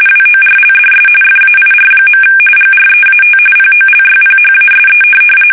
RTTY, about 45.45 baud. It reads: WORLDS FAIR. GREAT POLYPHASE
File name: fsk850 / benjamin-g-lamme